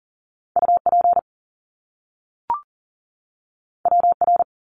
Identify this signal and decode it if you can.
Morse code, heard as a keyed tone, about 40 wpm, 700 Hz. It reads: UP WR
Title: Top Two Letter Combinations in Words 40wpm